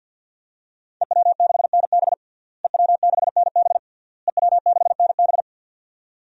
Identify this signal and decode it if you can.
Morse code, heard as a keyed tone, about 49 wpm, 700 Hz. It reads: EW6NB EW6NB EW6NB
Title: DXCC-100 Call Signs - 5 Characters - Repeated 3x - 50wpm